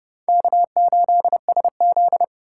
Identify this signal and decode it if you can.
Morse code, heard as a keyed tone, about 28 wpm, 700 Hz. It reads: K8SZ